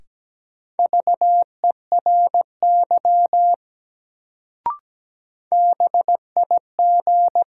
Morse code, heard as a keyed tone, about 17 wpm, 700 Hz. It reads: VERY BIG
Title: Sentences from Top 200 Words - New Word Drill - 17wpm